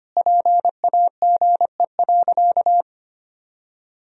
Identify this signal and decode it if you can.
Morse code, heard as a keyed tone, about 25 wpm, 700 Hz. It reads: PAGE.
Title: Sentences from Top 1000 Words - Review Part 1 - 25wpm